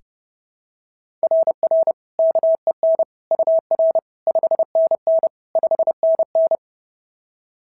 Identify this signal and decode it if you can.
Morse code, heard as a keyed tone, about 28 wpm, 660 Hz. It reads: RR KEN UR 5NN 5NN
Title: SOTA QSO Practice 30wpm